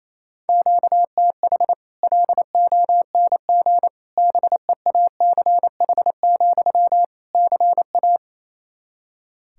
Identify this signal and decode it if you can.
Morse code, heard as a keyed tone, about 28 wpm, 700 Hz. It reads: QTH LONG BEACH, CA
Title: Practice QSO - Exchange 1 - Beginning - 28wpm